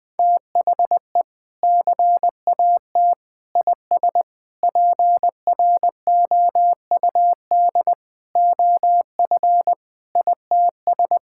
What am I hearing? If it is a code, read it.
Morse code, heard as a keyed tone, about 21 wpm, 700 Hz. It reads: THE CAT IS PROUD OF ITS
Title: Sentences with 2-Word Phrases - 20wpm